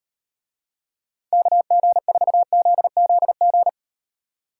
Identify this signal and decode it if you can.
Morse code, heard as a keyed tone, about 36 wpm, 700 Hz. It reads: KG4ZZG